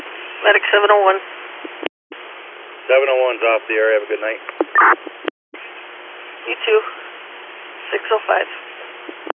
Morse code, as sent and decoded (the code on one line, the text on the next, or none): none